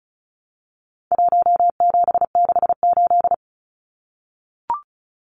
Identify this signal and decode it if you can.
Morse code, heard as a keyed tone, about 34 wpm, 700 Hz. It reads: 1768
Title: Numbers - Contest Style - 4-Digit 35wpm